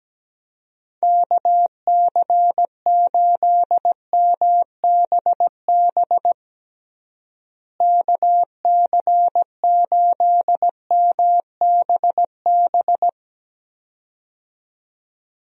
Morse code, heard as a keyed tone, about 17 wpm, 700 Hz. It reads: KC8MBB KC8MBB
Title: USA Call Signs - 6 Characters - Repeated 2x - 17wpm